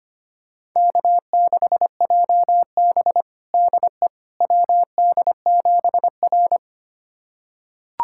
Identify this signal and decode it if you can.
Morse code, heard as a keyed tone, about 25 wpm, 700 Hz. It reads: K6JB DE WD7R
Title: USA Callsign DE Callsign - 4 Characters - 25wpm